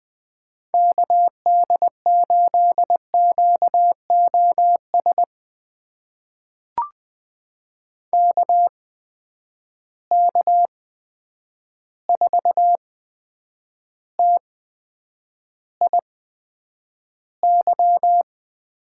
Morse code, heard as a keyed tone, about 20 wpm, spaced as for 9 wpm, 700 Hz. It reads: KD8QOS KK4TIY
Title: USA Call Signs - 8x Character Spacing - 20wpm